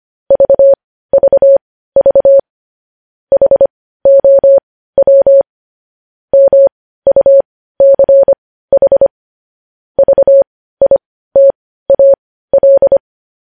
Morse code, heard as a keyed tone, about 25 wpm, spaced as for 15 wpm, 550 Hz. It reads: VVV HOW MUCH VITAL
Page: news headlines in morse code at 15 wpm